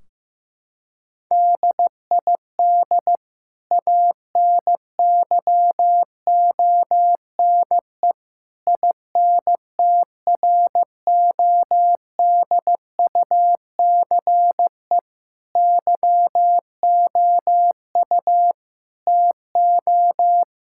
Morse code, heard as a keyed tone, about 15 wpm, 700 Hz. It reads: DID ANYONE INTRODUCE YOU TO
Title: Sentences from Top 1600 Words 15wpm